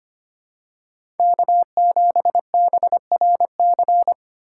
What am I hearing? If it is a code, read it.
Morse code, heard as a keyed tone, about 25 wpm, 700 Hz. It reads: K7BRC
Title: USA Call Signs - 5 Characters - 2x Character Spacing - 25wpm